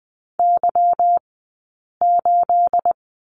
Morse code, heard as a keyed tone, about 20 wpm, spaced as for 8 wpm, 700 Hz. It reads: Y8
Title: 211 Lesson 34d - Number 8 - Two Characters 20wpm